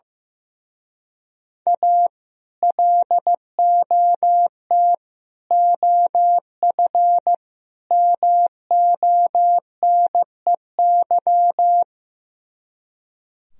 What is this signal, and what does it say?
Morse code, heard as a keyed tone, about 15 wpm, 700 Hz. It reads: A LOT OF MONEY
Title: Speed-Racing - Sets of 4 Words 15wpm